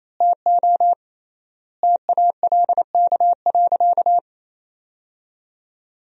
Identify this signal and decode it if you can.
Morse code, heard as a keyed tone, about 28 wpm, 700 Hz. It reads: TO TALK.
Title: 3x Word-Spacing - Sentences from Top 400 Words - 28wpm